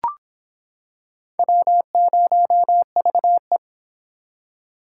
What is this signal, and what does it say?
Morse code, heard as a keyed tone, about 25 wpm, 700 Hz. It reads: W0VE